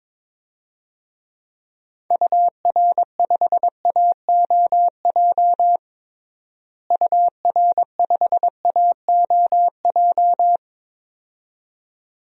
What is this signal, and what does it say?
Morse code, heard as a keyed tone, about 22 wpm, 700 Hz. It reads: UR5AOJ UR5AOJ
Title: DXCC-100 Call Signs - 6 Characters - Repeated 2x - 22wpm